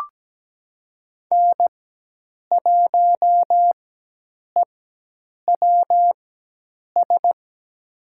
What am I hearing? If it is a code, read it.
Morse code, heard as a keyed tone, about 17 wpm, spaced as for 8 wpm, 700 Hz. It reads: N1EWS
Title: USA Call Signs - 5 Characters - 4x Character Spacing - 17wpm